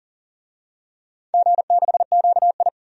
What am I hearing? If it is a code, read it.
Morse code, heard as a keyed tone, about 37 wpm, 700 Hz. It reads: G6QI
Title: DXCC-100 Call Signs - 4 Characters - 40wpm